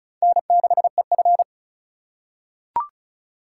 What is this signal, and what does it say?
Morse code, heard as a keyed tone, about 36 wpm, 700 Hz. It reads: N6EF
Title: USA Call Signs - 4x Character Spacing - 35wpm